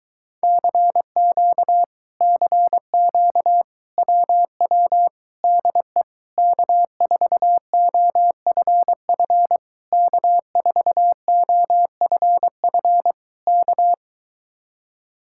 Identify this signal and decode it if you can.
Morse code, heard as a keyed tone, about 23 wpm, 700 Hz. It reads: CQ CQ WW DE K4OFF K4OFF K